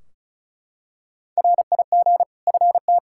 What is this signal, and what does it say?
Morse code, heard as a keyed tone, about 36 wpm, 700 Hz. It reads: RIG FT
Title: Practice QSO - Exchange 2 - Info - 35wpm